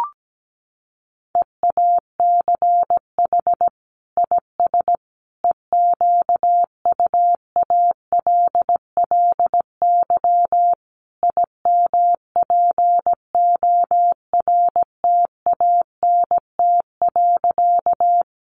Morse code, heard as a keyed tone, about 17 wpm, 700 Hz. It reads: EACH IS EQUALLY IMPORTANT.
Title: Sentences from Top 1400 Words 17wpm